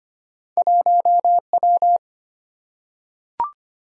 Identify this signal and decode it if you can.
Morse code, heard as a keyed tone, about 25 wpm, 700 Hz. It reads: 1W
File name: Two Letter-Number Combinations - 25wpm